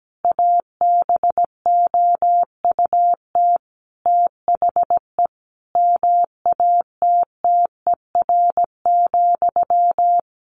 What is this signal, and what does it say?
Morse code, heard as a keyed tone, about 17 wpm, 700 Hz. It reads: ABOUT THE MATTER,
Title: Sentences from Top 700 Words 17wpm